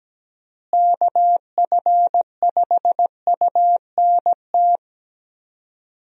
Morse code, heard as a keyed tone, about 17 wpm, 700 Hz. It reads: KF5UNT
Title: USA Call Signs - 6 Characters - Repeated 3x - 17wpm